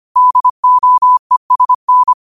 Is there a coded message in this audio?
Yes: Morse code from 0.2 to 2.1 s, heard as a keyed tone, about 25 wpm, 1000 Hz. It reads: DOESN